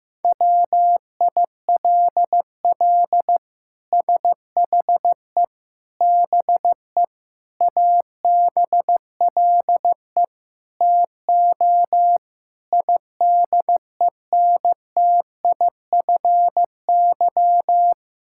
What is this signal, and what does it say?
Morse code, heard as a keyed tone, about 15 wpm, 700 Hz. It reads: WILL SHE BE ABLE TO IDENTIFY
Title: Sentences from Top 1000 Words - Review Part 6 - 15wpm